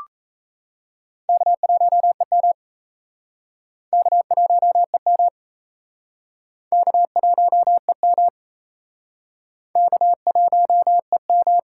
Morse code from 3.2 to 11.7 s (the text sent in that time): K1EM K1EM K1EM